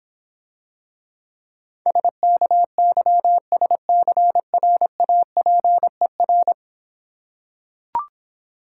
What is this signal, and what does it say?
Morse code, heard as a keyed tone, about 26 wpm, 700 Hz. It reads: SKYSCRAPER